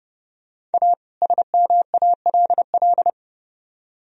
Morse code, heard as a keyed tone, about 30 wpm, 700 Hz. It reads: A SMALL